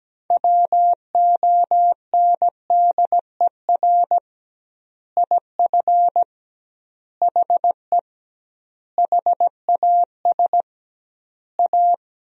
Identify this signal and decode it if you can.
Morse code, heard as a keyed tone, about 17 wpm, 700 Hz. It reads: WONDER IF HE HAS A
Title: 2x Word-Spacing - Sentences from Top 500 Words - 17wpm